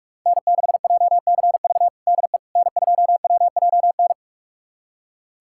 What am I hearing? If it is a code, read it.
Morse code, heard as a keyed tone, about 41 wpm, 700 Hz. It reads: N6JCV DE N2WJN
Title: USA Callsign DE Callsign - 5 Characters - 45wpm